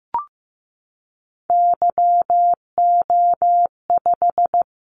Morse code, heard as a keyed tone, about 15 wpm, 700 Hz. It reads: YO5
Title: Callsign Prefixes for Top 25 DXCC - Repeated 3x - 15wpm